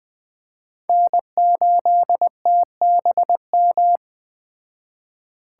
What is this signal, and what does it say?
Morse code, heard as a keyed tone, about 21 wpm, 700 Hz. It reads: N8TBM
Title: Speed Racing - USA 1x3 Call Signs 17wpm